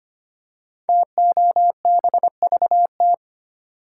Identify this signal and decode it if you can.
Morse code, heard as a keyed tone, about 25 wpm, 700 Hz. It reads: TOBVT
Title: Code Groups - 5 Letters - 25wpm